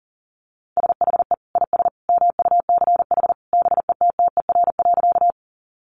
Morse code, heard as a keyed tone, about 40 wpm, 700 Hz. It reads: SHE IS MUCH BETTER.
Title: Sentences from Top 400 Words 40wpm